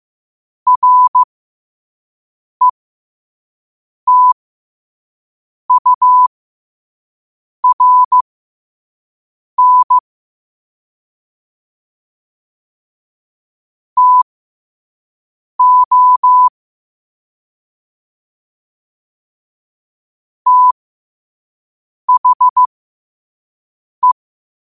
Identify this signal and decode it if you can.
Morse code, heard as a keyed tone, about 15 wpm, spaced as for 5 wpm, 1000 Hz. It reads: RETURN TO THE
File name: qotdfiles / 5wpm